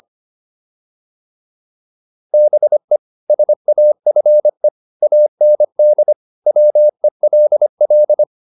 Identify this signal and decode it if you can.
Morse code, heard as a keyed tone, about 25 wpm, 600 Hz. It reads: BE SAFE AND WELL